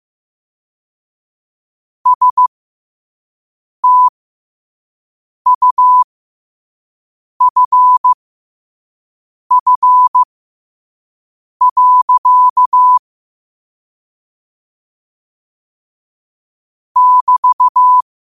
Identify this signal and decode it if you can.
Morse code, heard as a keyed tone, about 15 wpm, spaced as for 5 wpm, 1000 Hz. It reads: STUFF. <BT>
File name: qotdfiles / 5wpm